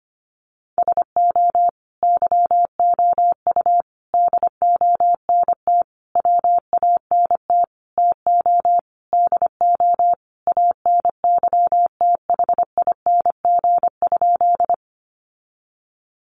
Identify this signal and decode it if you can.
Morse code, heard as a keyed tone, about 25 wpm, 700 Hz. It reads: SO YOU DONT WANT TO DO ANYTHING?
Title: Sentences from Top 1000 Words 25wpm